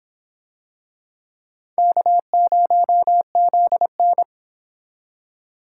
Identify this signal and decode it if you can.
Morse code, heard as a keyed tone, about 24 wpm, 700 Hz. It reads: K0ZN